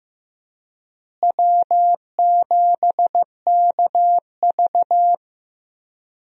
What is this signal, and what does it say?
Morse code, heard as a keyed tone, about 15 wpm, 700 Hz. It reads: W7KV
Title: USA Call Signs - 4 Characters - Repeated 2x - 15wpm